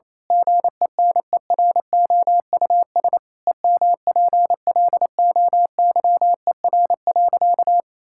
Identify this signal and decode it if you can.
Morse code, heard as a keyed tone, about 27 wpm, 700 Hz. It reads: GENEROUS EMPLOYER.
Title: Pangram Challenge v2 - 28wpm